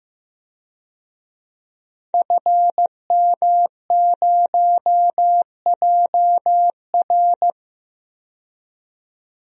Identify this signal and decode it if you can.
Morse code, heard as a keyed tone, about 15 wpm, 700 Hz. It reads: FM0JR
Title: DXCC-100 Call Signs - 5 Characters - 15wpm